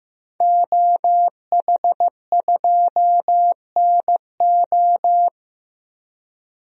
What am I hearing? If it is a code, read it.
Morse code, heard as a keyed tone, about 15 wpm, 700 Hz. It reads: OH2NO
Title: DXCC-100 Call Signs - 5 Characters - Repeated 2x - 15wpm